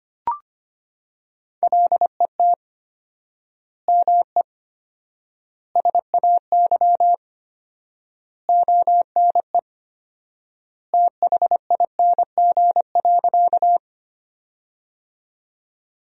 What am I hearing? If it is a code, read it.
Morse code, heard as a keyed tone, about 25 wpm, 700 Hz. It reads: LET ME SAY ONE THING.
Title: 4x Word-Spacing - Sentences from Top 300 Words - 25wpm